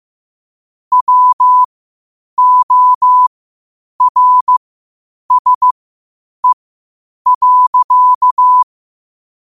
Morse code, heard as a keyed tone, about 15 wpm, spaced as for 8 wpm, 1000 Hz. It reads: WORSE.
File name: qotdfiles / 8wpm